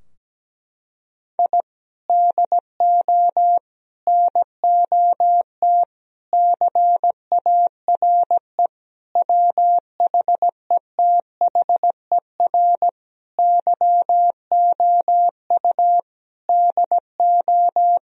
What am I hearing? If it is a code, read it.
Morse code, heard as a keyed tone, about 17 wpm, 700 Hz. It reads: I DO NOT CARE WHETHER YOU DO
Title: Sentences from Top 1000 Words - Review Part 3 - 17wpm